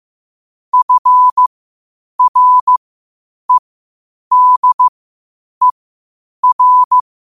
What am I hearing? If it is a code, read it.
Morse code, heard as a keyed tone, about 15 wpm, spaced as for 8 wpm, 1000 Hz. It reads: FREDER
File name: qotdfiles / 8wpm